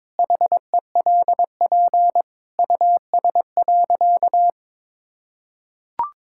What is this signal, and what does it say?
Morse code, heard as a keyed tone, about 22 wpm, 700 Hz. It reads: HELP US.
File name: practice / 22wpm / Sentences with 2-Word Phrases - 22wpm